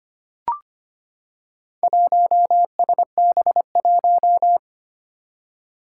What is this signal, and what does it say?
Morse code, heard as a keyed tone, about 25 wpm, 700 Hz. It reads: 1SB1
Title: Four Challenging Letter-Number Combinations 25wpm